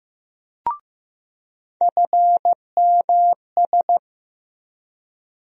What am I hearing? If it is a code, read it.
Morse code, heard as a keyed tone, about 15 wpm, 700 Hz. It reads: FMS
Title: USA Call Sign Postfixes - 3 Letters 15wpm